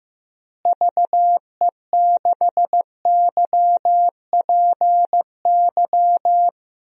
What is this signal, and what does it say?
Morse code, heard as a keyed tone, about 14 wpm, 700 Hz. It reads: VE6YPY
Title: DXCC-100 Call Signs - 6 Characters - Repeated 2x - 15wpm